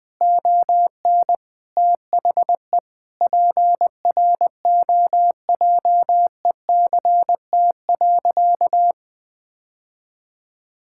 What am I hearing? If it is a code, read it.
Morse code, heard as a keyed tone, about 21 wpm, 700 Hz. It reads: ON THE PROJECT.
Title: Sentences with 5-Word Phrases - Spoken Beginning - 20wpm